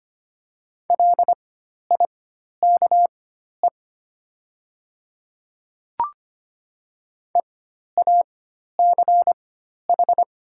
Morse code, heard as a keyed tone, about 25 wpm, spaced as for 12 wpm, 700 Hz. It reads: LIKE EACH
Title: 4x Character Spacing - Top 200 Words - 25wpm